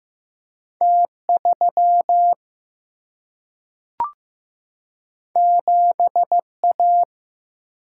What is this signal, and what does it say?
Morse code, heard as a keyed tone, about 15 wpm, 700 Hz. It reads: T 3 7 A
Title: Single-Letter Single-Number 15wpm